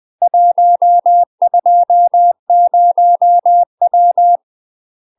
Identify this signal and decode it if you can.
Morse code, heard as a keyed tone, about 20 wpm, 700 Hz. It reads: 120W
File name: UK MASTER QSOs TRAINING / 20WPM / UK MASTER QSOs 0028